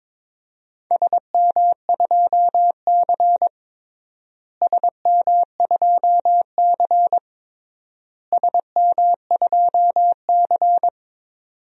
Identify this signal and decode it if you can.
Morse code, heard as a keyed tone, about 21 wpm, 700 Hz. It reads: SM2C SM2C SM2C
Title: DXCC-100 Call Signs - 4 Characters - Repeated 3x - 22wpm